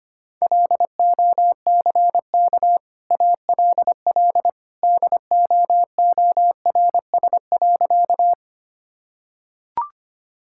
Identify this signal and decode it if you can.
Morse code, heard as a keyed tone, about 25 wpm, 700 Hz. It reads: LOCK ALL DOORS.
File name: Easy Sentences with Hints - 25wpm